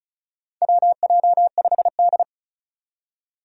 Morse code, heard as a keyed tone, about 36 wpm, 700 Hz. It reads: WJ5D
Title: USA Call Signs - 4 Characters - 35wpm